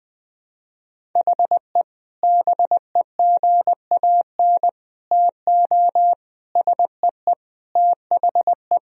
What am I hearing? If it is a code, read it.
Morse code, heard as a keyed tone, about 20 wpm, 700 Hz. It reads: HE BEGAN TO SEE THE
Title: Sentences from Top 1000 Words - Review Part 1 - 20wpm